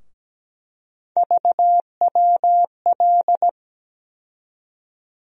Morse code, heard as a keyed tone, about 17 wpm, 700 Hz. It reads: VWL
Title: USA Call Sign Postfixes - 3 Letters 17wpm